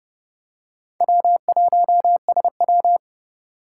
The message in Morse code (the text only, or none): W1SW